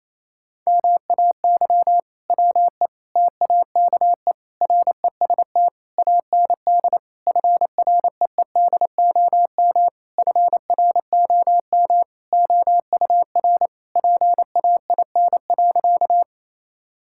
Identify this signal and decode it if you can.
Morse code, heard as a keyed tone, about 28 wpm, 700 Hz. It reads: MAY WE TAKE REST AND FREEDOM FROM OUR PAIN.
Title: Sentences from Top 1600 Words 28wpm